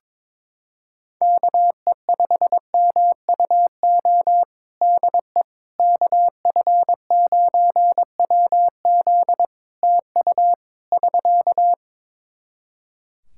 Morse code, heard as a keyed tone, about 22 wpm, 700 Hz. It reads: KE5MUO DE KF9WZ TU <SK>
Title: K1USN SST QSO Practice 22wpm